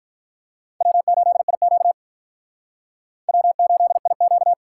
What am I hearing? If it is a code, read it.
Morse code, heard as a keyed tone, about 53 wpm, 700 Hz. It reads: W8IQ W8IQ